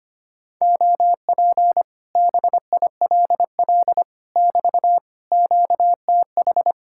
Morse code, heard as a keyed tone, about 24 wpm, 700 Hz. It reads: OP BILL <BT> QTH
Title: Speed-Racing Common QSO Phrases Encore - 25wpm